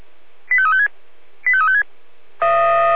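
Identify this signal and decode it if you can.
Five-tone selective call (ZVEI), heard as quick runs of five tones, tones of about 70 ms. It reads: 85356 85356
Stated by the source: ZVEI five-tone call 85265, received on SDR.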